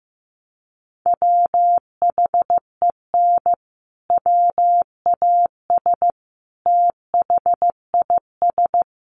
Morse code, heard as a keyed tone, about 15 wpm, 700 Hz. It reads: WHEN WAS THIS
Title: Sentences from Top 1200 Words 15wpm